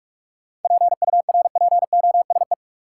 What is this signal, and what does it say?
Morse code, heard as a keyed tone, about 46 wpm, 700 Hz. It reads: PURPOSE
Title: Sentences from Top 800 Words - New Word Drill - 45wpm